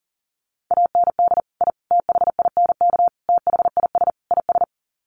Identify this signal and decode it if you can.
Morse code, heard as a keyed tone, about 40 wpm, 700 Hz. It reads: AND I THINK THIS IS
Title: Sets of 5 Words Encore 40wpm